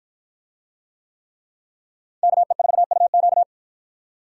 Morse code, heard as a keyed tone, about 50 wpm, 700 Hz. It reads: KE4UQ